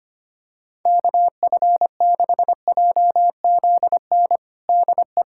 Morse code, heard as a keyed tone, about 25 wpm, 700 Hz. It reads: KF6JZN DE
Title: USA Callsign DE Callsign - 6 Characters - 25wpm